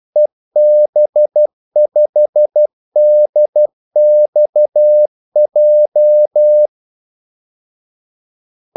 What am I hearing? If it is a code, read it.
Morse code, heard as a keyed tone, about 12 wpm, 600 Hz. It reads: EB5DXJ
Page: 2 ekstra ordmellemrum mellem kaldesignalerne